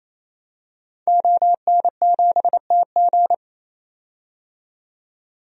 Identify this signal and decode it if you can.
Morse code, heard as a keyed tone, about 28 wpm, 700 Hz. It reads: ON7TG
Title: DXCC-100 Call Signs - 5 Characters - 28wpm